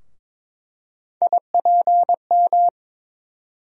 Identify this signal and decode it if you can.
Morse code, heard as a keyed tone, about 21 wpm, 700 Hz. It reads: IPM
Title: USA Call Sign Postfixes - 3 Letters 22wpm